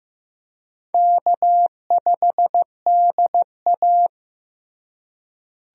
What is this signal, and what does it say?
Morse code, heard as a keyed tone, about 15 wpm, 700 Hz. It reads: K5DA
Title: USA Call Signs - 4 Characters - Repeated 2x - 15wpm